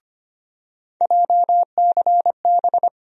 Morse code, heard as a keyed tone, about 25 wpm, 700 Hz. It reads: JCB
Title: USA Call Sign Postfixes - 3 Letters 25wpm